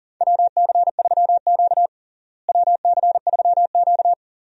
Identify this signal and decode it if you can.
Morse code, heard as a keyed tone, about 40 wpm, 700 Hz. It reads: WC3Q WC3Q
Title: USA Call Signs - 4 Characters - Repeated 2x - 40wpm